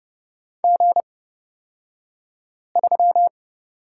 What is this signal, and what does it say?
Morse code, heard as a keyed tone, about 30 wpm, spaced as for 5 wpm, 700 Hz. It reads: G3
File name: Lesson 38d - Letter q - Two Characters 30wpm